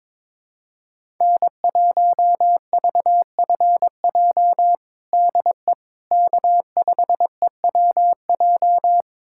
Morse code, heard as a keyed tone, about 22 wpm, 700 Hz. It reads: N1VFJ DE K5EWJ
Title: USA Callsign DE Callsign - 5 Characters - 22wpm